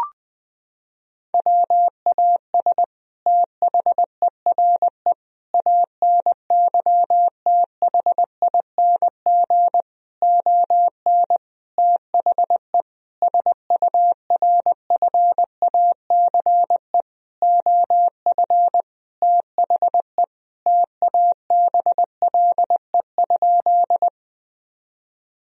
Morse code, heard as a keyed tone, about 20 wpm, 700 Hz. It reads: WAS THERE ANYTHING ON THE SURFACE OF THE TABLE?
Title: Sentences from Top 1000 Words - Review Part 3 - 20wpm